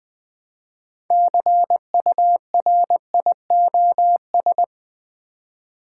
Morse code, heard as a keyed tone, about 20 wpm, 700 Hz. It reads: CURIOS
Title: Build A Word - 3 Letters At A Time - 9-Letter Words - 20wpm